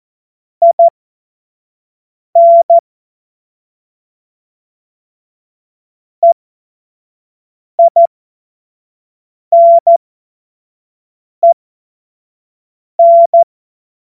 Morse code, heard as a keyed tone, about 14 wpm, spaced as for 5 wpm, 680 Hz. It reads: IN EINEN
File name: Morsekurs / Witze / Witz 01. Februar 2022 25 F70